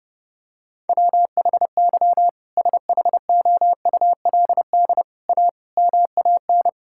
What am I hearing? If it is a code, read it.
Morse code, heard as a keyed tone, about 30 wpm, 700 Hz. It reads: WHY SHOULD A MAN
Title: Sentences from Top 1600 Words 30wpm